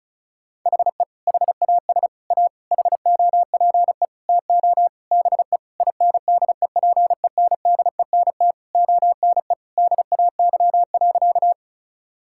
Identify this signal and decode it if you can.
Morse code, heard as a keyed tone, about 34 wpm, 700 Hz. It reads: HE HAS A HOPE TO BE INDEPENDENT ONE DAY.
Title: Sentences from Top 1000 Words - Review Part 6 - 35wpm